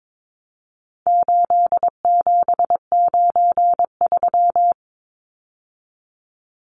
Morse code, heard as a keyed tone, about 22 wpm, 700 Hz. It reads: 8793
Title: Numbers - 4-Digit 22wpm